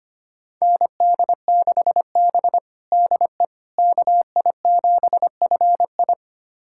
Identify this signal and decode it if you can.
Morse code, heard as a keyed tone, about 25 wpm, 700 Hz. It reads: ND6B DE KI7FI
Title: USA Callsign DE Callsign 25wpm